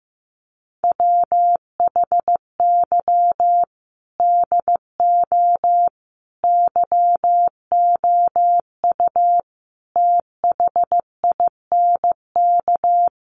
Morse code, heard as a keyed tone, about 15 wpm, 700 Hz. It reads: WHY DO YOU THINK
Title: Sentences from Top 1400 Words 15wpm